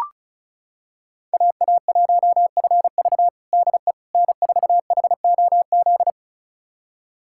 Morse code, heard as a keyed tone, about 36 wpm, 700 Hz. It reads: AA1FV DE N4HOZ
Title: USA Callsign DE Callsign - 5 Characters - 35wpm